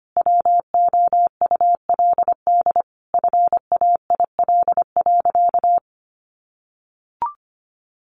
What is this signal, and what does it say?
Morse code, heard as a keyed tone, about 25 wpm, 700 Hz. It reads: WOULD FAIL.
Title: Sentences from Top 1100 Words 25wpm